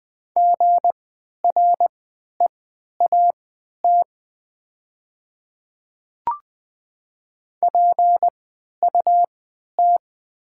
Morse code, heard as a keyed tone, about 20 wpm, spaced as for 11 wpm, 700 Hz. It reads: GREAT PUT
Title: 3x Character Spacing - Top 200 Words - 20wpm